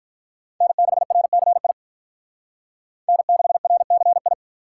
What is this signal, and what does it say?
Morse code, heard as a keyed tone, about 42 wpm, 700 Hz. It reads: N6RCI N6RCI
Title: Speed Racing - USA 1x3 Call Signs 35wpm